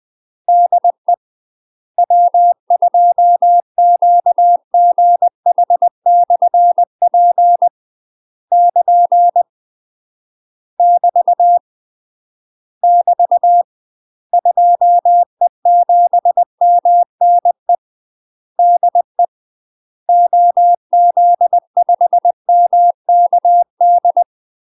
Morse code, heard as a keyed tone, about 20 wpm, 700 Hz. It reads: DE W2QGH/P <KN> <BT> <BT> 2E7MNE DE OZ5MKD
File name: UK MASTER QSOs TRAINING / 20WPM / UK MASTER QSOs 0016